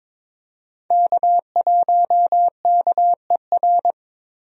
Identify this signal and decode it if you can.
Morse code, heard as a keyed tone, about 22 wpm, 700 Hz. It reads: K1KER
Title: USA Call Signs - 5 Characters - 22wpm